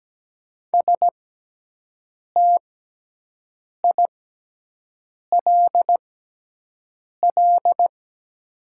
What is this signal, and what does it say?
Morse code, heard as a keyed tone, about 17 wpm, spaced as for 6 wpm, 700 Hz. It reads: STILL
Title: 6x Character Spacing - Top 300 Words - 17wpm